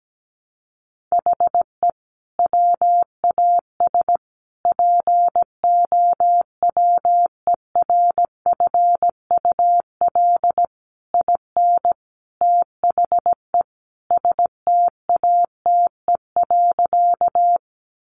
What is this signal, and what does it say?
Morse code, heard as a keyed tone, about 17 wpm, 700 Hz. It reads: HE WAS POWERFUL IN THE STATE.
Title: Sentences from Top 1300 Words 17wpm